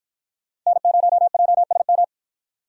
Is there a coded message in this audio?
Yes: Morse code from 0.7 to 2.0 s, heard as a keyed tone, about 48 wpm, 700 Hz. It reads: N0JSM